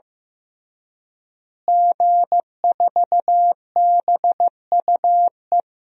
Morse code, heard as a keyed tone, about 15 wpm, 700 Hz. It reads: G4BUE
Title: DXCC-100 Call Signs - 5 Characters - Repeated 3x - 15wpm